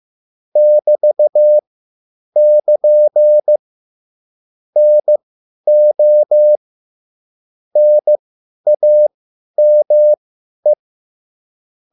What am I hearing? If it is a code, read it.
Morse code, heard as a keyed tone, about 15 wpm, spaced as for 10 wpm, 600 Hz. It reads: <BT><KN> NO NAME